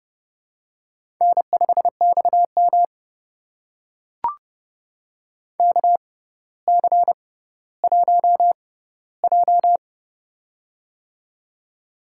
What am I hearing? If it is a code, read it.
Morse code, heard as a keyed tone, about 30 wpm, spaced as for 15 wpm, 700 Hz. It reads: N5XM KC1J
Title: USA Call Signs - 4 Characters - 6x Character Spacing - 30wpm